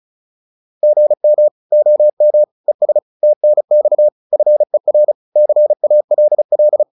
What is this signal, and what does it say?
Morse code, heard as a keyed tone, about 34 wpm, 600 Hz. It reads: GM OM ES TNX FER CALL
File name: QSO208 35